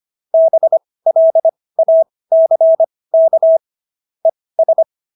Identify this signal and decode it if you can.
Morse code, heard as a keyed tone, about 25 wpm, spaced as for 18 wpm, 650 Hz. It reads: BLACK ES